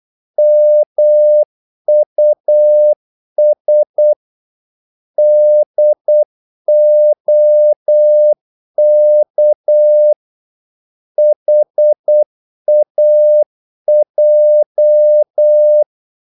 Morse code, heard as a keyed tone, about 8 wpm, 600 Hz. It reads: MUS DOK HAJ
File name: Korte ord DK 8wpm